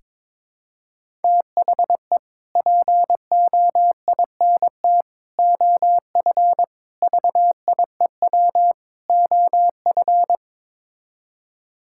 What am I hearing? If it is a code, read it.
Morse code, heard as a keyed tone, about 22 wpm, 700 Hz. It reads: THE POINT OF VIEW OF
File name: practice / 22wpm / Build A Phrase - 5-Word Phrases - 22wpm